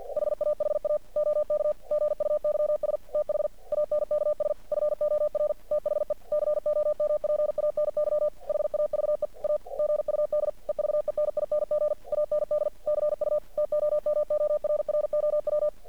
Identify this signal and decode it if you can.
Morse code, heard as a keyed tone, about 50 wpm, 600 Hz. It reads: HAHA OK GUYS THANKS FOR THE COMPANY HAVE A GUD EVENING AND CU TOMORROW
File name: archived / CRSnet